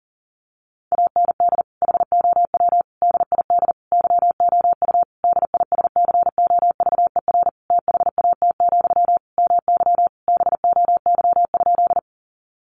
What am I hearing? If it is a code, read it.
Morse code, heard as a keyed tone, about 40 wpm, 700 Hz. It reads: AND HOW DID YOU DISCOVER THAT, MY BOY?
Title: Sentences from Top 1100 Words 40wpm